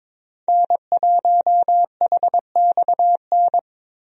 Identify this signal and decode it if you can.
Morse code, heard as a keyed tone, about 22 wpm, 700 Hz. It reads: N1HXN
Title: USA Call Signs - 5 Characters - Repeated 3x - 22wpm